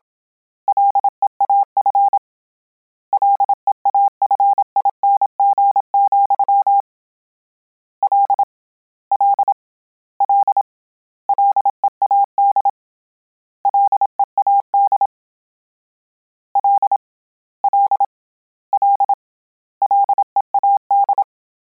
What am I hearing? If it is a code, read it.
Morse code, heard as a keyed tone, about 27 wpm, 800 Hz. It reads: LEAF LEAFING, L L L LEAD LEAD L L L LEAD